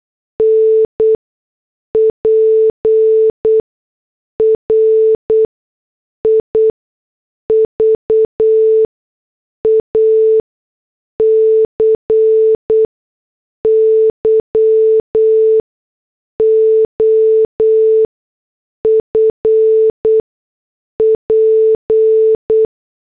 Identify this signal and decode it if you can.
Morse code, heard as a keyed tone, about 8 wpm, spaced as for 6 wpm, 440 Hz. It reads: N PRIVACY OF P
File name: morse / headlines